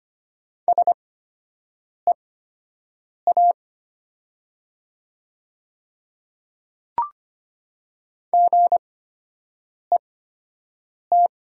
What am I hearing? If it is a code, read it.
Morse code, heard as a keyed tone, about 25 wpm, spaced as for 7 wpm, 700 Hz. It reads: SEA GET
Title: 8x Character Spacing - Top 300 Words - 25wpm